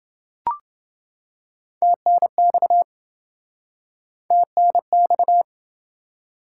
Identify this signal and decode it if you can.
Morse code, heard as a keyed tone, about 27 wpm, 700 Hz. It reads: TNX TNX